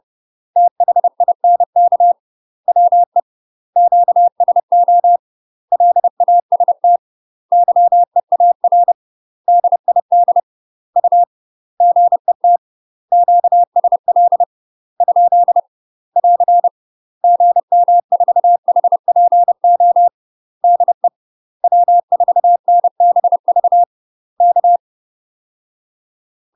Morse code, heard as a keyed tone, about 30 wpm, 700 Hz. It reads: THINK WE QSO LAST YEAR DID U GET QSL ? <AR> GM4HPO DE W4NBV K
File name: UK MASTER QSOs TRAINING / 30WPM / UK MASTER QSOs 0035